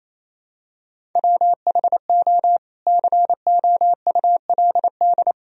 Morse code, heard as a keyed tone, about 28 wpm, 700 Hz. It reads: WHO COULD
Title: Sentences from Top 300 Words and Abbreviations 28wpm